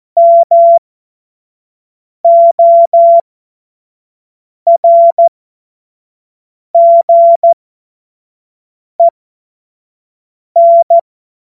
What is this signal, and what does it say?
Morse code, heard as a keyed tone, about 14 wpm, spaced as for 5 wpm, 680 Hz. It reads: MORGEN